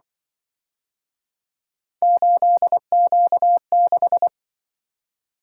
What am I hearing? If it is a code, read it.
Morse code, heard as a keyed tone, about 25 wpm, 700 Hz. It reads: 8Q6
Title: Speed-Racing - Three Letter-Number Combinations - 20wpm